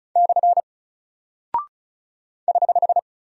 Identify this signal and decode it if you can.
Morse code, heard as a keyed tone, about 36 wpm, 700 Hz. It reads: / <HH>
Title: Single Character (Letter:Number:Punctuation:Prosign) Alt 35wpm